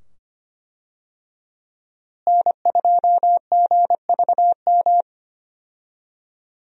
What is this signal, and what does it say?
Morse code, heard as a keyed tone, about 25 wpm, 700 Hz. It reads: N2GVM
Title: USA Call Signs - 5 Characters - Repeated 2x - 25wpm